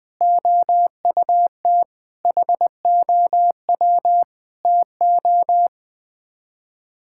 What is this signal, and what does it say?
Morse code, heard as a keyed tone, about 20 wpm, 700 Hz. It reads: OUT HOW TO
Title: Sets of 6 Words - 20wpm